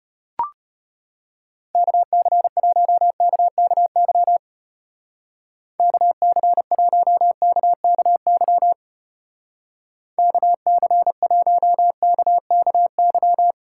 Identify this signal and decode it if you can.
Morse code, heard as a keyed tone, about 32 wpm, 700 Hz. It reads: KC1KKY KC1KKY KC1KKY